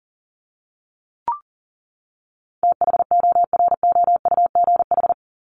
Morse code, heard as a keyed tone, about 40 wpm, 700 Hz. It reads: THOROUGH
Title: LY Suffix - Top 100 Words with Base Words 40wpm